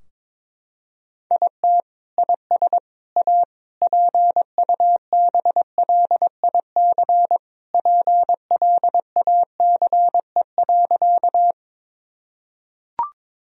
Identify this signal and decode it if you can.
Morse code, heard as a keyed tone, about 22 wpm, 700 Hz. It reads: IT IS A PUBLIC PLACE.
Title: Sentences from Top 1000 Words - Review Part 3 - 22wpm